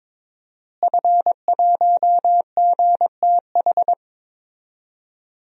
Morse code, heard as a keyed tone, about 22 wpm, 700 Hz. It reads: F1GTH
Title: DXCC-100 Call Signs - 5 Characters - Repeated 3x - 22wpm